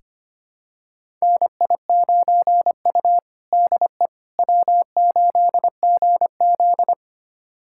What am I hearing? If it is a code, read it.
Morse code, heard as a keyed tone, about 25 wpm, 700 Hz. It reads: NI9U DE W8GZ
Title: USA Callsign DE Callsign - 4 Characters - 25wpm